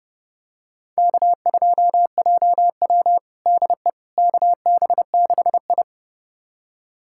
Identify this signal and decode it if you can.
Morse code, heard as a keyed tone, about 30 wpm, 700 Hz. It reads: K2JW DE KB6I
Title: USA Callsign DE Callsign - 4 Characters - 30wpm